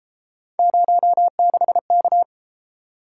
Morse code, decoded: 06K

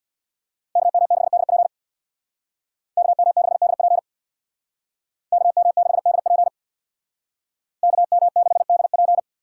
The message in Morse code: KM6DP KM6DP KM6DP KM6DP

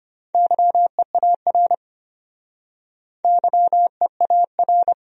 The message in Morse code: YEAR YEAR

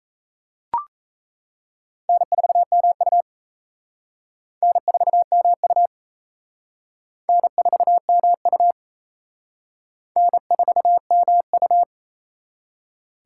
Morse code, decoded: N4MU N4MU N4MU N4MU

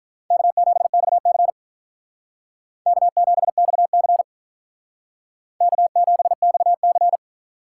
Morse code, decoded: K7XC K7XC K7XC

